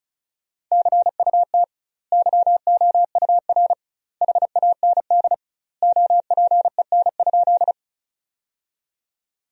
Morse code, heard as a keyed tone, about 34 wpm, 700 Hz. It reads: CUT YOUR HAND OPEN?
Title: 2x Word-Spacing - Sentences from Top 300 Words - 35wpm